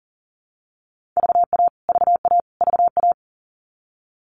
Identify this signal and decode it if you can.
Morse code, heard as a keyed tone, about 40 wpm, 700 Hz. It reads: VA VA VA
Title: US States Repeated 3x 40wpm